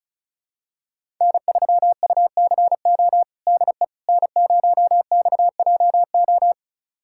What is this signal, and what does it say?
Morse code, heard as a keyed tone, about 36 wpm, 700 Hz. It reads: N3UCO DE N0XJO
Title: USA Callsign DE Callsign - 5 Characters - 35wpm